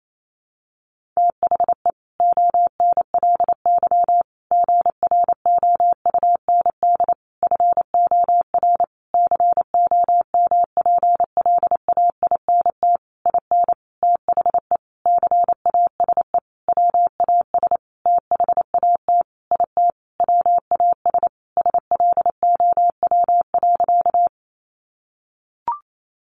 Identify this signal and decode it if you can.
Morse code, heard as a keyed tone, about 28 wpm, 700 Hz. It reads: THE ONLY GROUND FOR COMPLAINT IN THE CASE WAS THAT IT WAS SLOW.
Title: Sentences from Top 1400 Words 28wpm